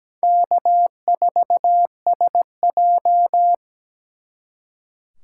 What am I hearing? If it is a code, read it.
Morse code, heard as a keyed tone, about 20 wpm, 700 Hz. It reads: K4SJ
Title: Speed Racing - USA 1x2 Call Signs 17wpm